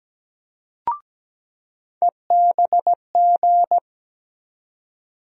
Morse code, heard as a keyed tone, about 17 wpm, 700 Hz. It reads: EBG
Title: USA Call Sign Postfixes - 3 Letters 17wpm